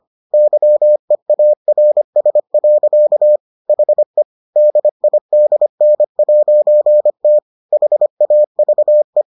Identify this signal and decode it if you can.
Morse code, heard as a keyed tone, about 25 wpm, 600 Hz. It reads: YEARS. HE DIDN'T HAVE